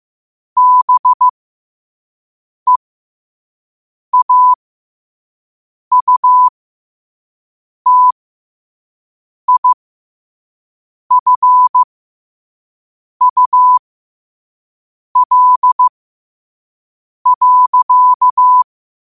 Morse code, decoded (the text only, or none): BEAUTIFUL.